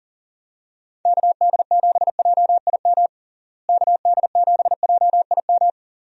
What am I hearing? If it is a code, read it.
Morse code, heard as a keyed tone, about 40 wpm, 700 Hz. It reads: KD7JIM KD7JIM
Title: USA Call Signs - 6 Characters - Repeated 3x - 40wpm